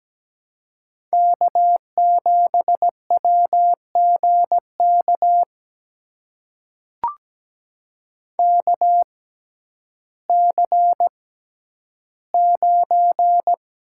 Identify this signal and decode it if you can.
Morse code, heard as a keyed tone, about 17 wpm, spaced as for 10 wpm, 700 Hz. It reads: K7WGK KC9
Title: USA Call Signs - 6x Character Spacing - 17wpm